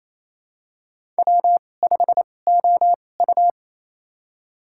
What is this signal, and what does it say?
Morse code, heard as a keyed tone, about 28 wpm, spaced as for 20 wpm, 700 Hz. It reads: W5OU
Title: USA Call Signs - 2x Character Spacing - 28wpm